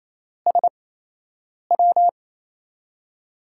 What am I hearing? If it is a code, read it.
Morse code, heard as a keyed tone, about 28 wpm, spaced as for 8 wpm, 700 Hz. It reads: SW
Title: USA Call Signs - 6 Characters - 8x Character Spacing - 28wpm